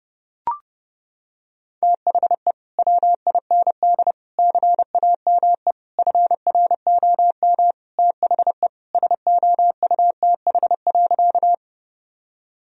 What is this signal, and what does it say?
Morse code, heard as a keyed tone, about 30 wpm, 700 Hz. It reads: THE WIND CAME FROM THE SOUTH.
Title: Sentences from Top 1000 Words - Review Part 2 - 30wpm